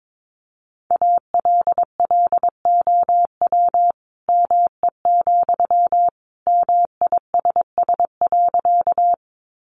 Morse code, heard as a keyed tone, about 22 wpm, 700 Hz. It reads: ALLOW ME, MISS.
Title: Sentences from Top 700 Words 22wpm